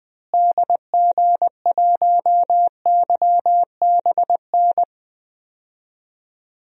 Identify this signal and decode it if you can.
Morse code, heard as a keyed tone, about 21 wpm, 700 Hz. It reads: DG1YBN
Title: DXCC-100 Call Signs - 6 Characters - Repeated 2x - 20wpm